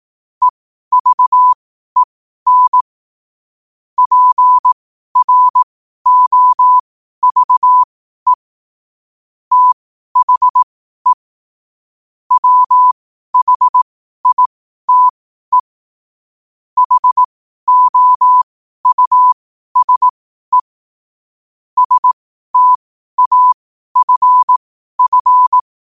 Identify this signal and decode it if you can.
Morse code, heard as a keyed tone, about 18 wpm, spaced as for 12 wpm, 1000 Hz. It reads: EVEN PROVE THE WHITE HOUSE STAFF
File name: qotdfiles / 12wpm